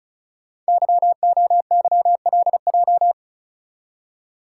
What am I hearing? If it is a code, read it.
Morse code, heard as a keyed tone, about 36 wpm, 700 Hz. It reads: YOYLJ
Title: Code Groups - 5 Letter:Numbers - 35wpm